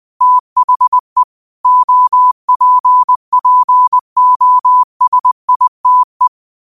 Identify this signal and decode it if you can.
Morse code, heard as a keyed tone, about 20 wpm, 1000 Hz. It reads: THE OPPOSITE